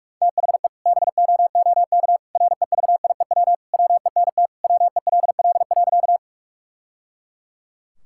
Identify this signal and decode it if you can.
Morse code, heard as a keyed tone, about 46 wpm, 690 Hz. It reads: THE BOOK REVIEW WENT WELL.
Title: Sentences from Top 1000 Words - Review Part 6 - 45wpm